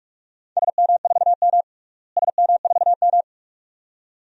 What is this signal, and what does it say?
Morse code, heard as a keyed tone, about 42 wpm, 700 Hz. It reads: SM4M SM4M
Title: DXCC-100 Call Signs - 4 Characters - Repeated 2x - 45wpm